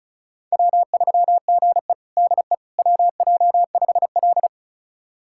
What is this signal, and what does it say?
Morse code, heard as a keyed tone, about 36 wpm, 700 Hz. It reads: W3GE DE WJ5L
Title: USA Callsign DE Callsign - 4 Characters - 35wpm